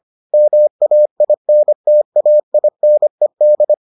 Morse code, heard as a keyed tone, about 25 wpm, 600 Hz. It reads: MAINTAINED